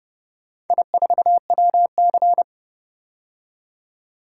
Morse code, heard as a keyed tone, about 29 wpm, 700 Hz. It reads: I4WC